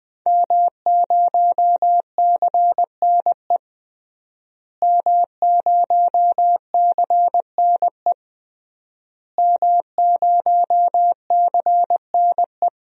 Morse code, heard as a keyed tone, about 20 wpm, 700 Hz. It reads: M0CNE M0CNE M0CNE
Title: DXCC-100 Call Signs - 5 Characters - Repeated 3x - 20wpm